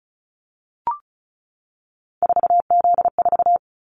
Morse code, heard as a keyed tone, about 34 wpm, 700 Hz. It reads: 4Z4
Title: Callsign Prefixes for Top 50 DXCC - 35wpm